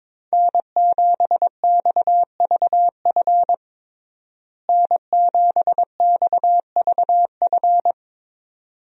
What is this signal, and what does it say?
Morse code, heard as a keyed tone, about 22 wpm, 700 Hz. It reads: N7XVF N7XVF
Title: USA Call Signs - 5 Characters - Repeated 2x - 22wpm